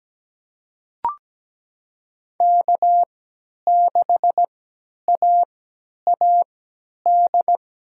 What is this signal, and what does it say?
Morse code, heard as a keyed tone, about 17 wpm, spaced as for 10 wpm, 700 Hz. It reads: K6AAD
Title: USA Call Signs - 5 Characters - 3x Character Spacing - 17wpm